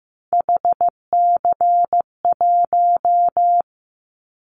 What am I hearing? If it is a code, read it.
Morse code, heard as a keyed tone, about 15 wpm, 700 Hz. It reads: HC1
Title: Callsign Prefixes for Top 100 DXCC - 15wpm